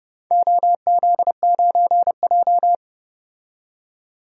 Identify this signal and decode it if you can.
Morse code, heard as a keyed tone, about 29 wpm, 700 Hz. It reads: OZ9J
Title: DXCC-100 Call Signs - 4 Characters - 30wpm